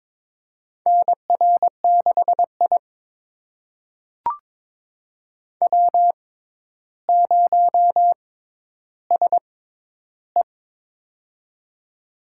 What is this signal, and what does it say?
Morse code, heard as a keyed tone, about 22 wpm, spaced as for 11 wpm, 700 Hz. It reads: NR6I W0SE
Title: USA Call Signs - 4 Characters - 6x Character Spacing - 22wpm